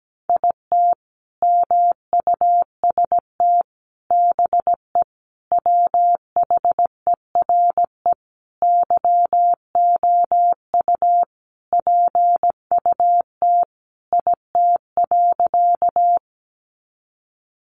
Morse code, heard as a keyed tone, about 17 wpm, 700 Hz. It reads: IT MUST BE WHERE YOU PUT IT.
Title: Sentences from Top 200 Words 17wpm